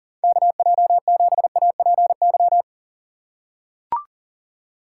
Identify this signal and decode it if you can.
Morse code, heard as a keyed tone, about 40 wpm, 700 Hz. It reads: KJ7APY
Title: USA Call Signs - 6x Character Spacing - 40wpm